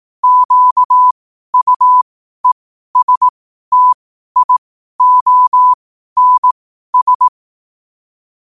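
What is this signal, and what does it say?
Morse code, heard as a keyed tone, about 18 wpm, spaced as for 12 wpm, 1000 Hz. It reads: QUESTIONS
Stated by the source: Morse code sent at 12 words per minute.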